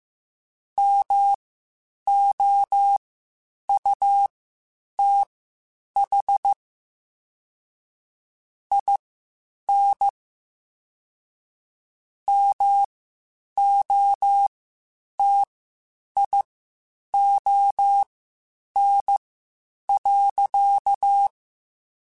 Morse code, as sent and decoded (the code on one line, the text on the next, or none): -- --- ..- - .... / .. -. / -- --- - .. --- -. .-.-.-
MOUTH IN MOTION.